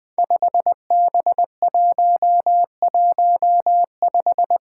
Morse code, heard as a keyed tone, about 20 wpm, 700 Hz. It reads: 5B115
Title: Five Challenging Letter-Number Combinations 20wpm